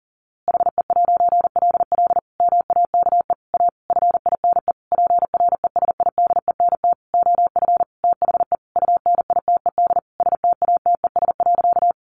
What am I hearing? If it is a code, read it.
Morse code, heard as a keyed tone, about 40 wpm, 700 Hz. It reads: HE'LL MAKE A FINE PRESIDENT OF THE UNITED STATES.
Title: A Sound Of Thunder 40wpm